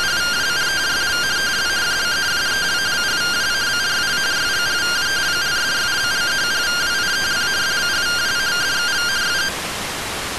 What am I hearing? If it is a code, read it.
Two-tone teleprinter signal, about 170 Hz shift.